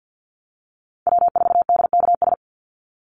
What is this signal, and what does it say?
Morse code, heard as a keyed tone, about 50 wpm, 700 Hz. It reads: W4DKS
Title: Call Signs - 50wpm